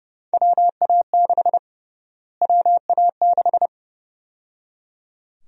Morse code, heard as a keyed tone, about 31 wpm, 700 Hz. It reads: WA6 WA6
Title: USA Call Sign Prefixes - Repeated 2x - 30wpm